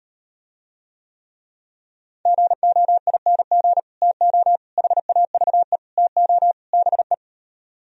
Morse code, heard as a keyed tone, about 38 wpm, 700 Hz. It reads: GOING TO HAVE TO BE